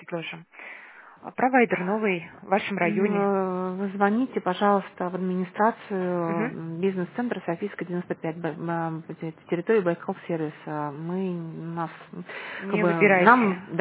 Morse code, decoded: none

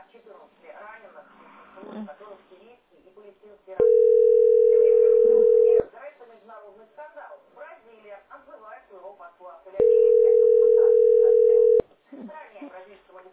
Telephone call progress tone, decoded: ringback tone

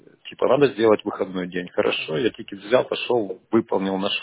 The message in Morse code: none

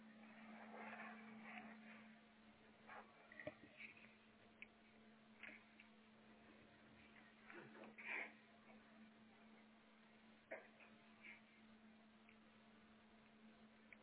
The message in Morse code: none

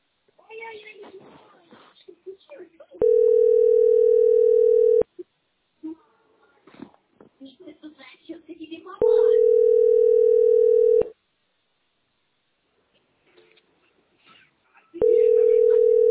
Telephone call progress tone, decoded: ringback tone

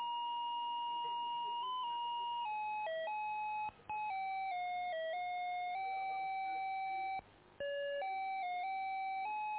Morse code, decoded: none

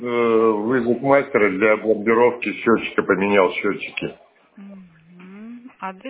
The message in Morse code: none

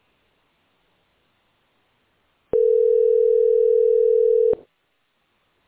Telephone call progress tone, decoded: ringback tone

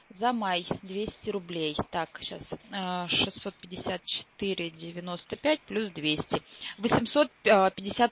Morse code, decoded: none

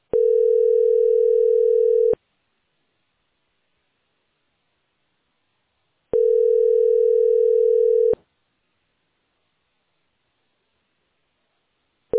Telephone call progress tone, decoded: ringback tone

